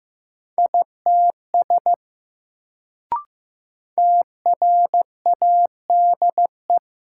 Morse code, heard as a keyed tone, about 15 wpm, 700 Hz. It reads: ITS TRADE
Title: Sentences from Top 600 Words - New Word Drill - 15wpm